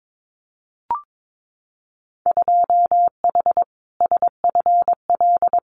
Morse code, heard as a keyed tone, about 22 wpm, 700 Hz. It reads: 2H SFL
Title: Winter Field Day Exchanges Encore 22wpm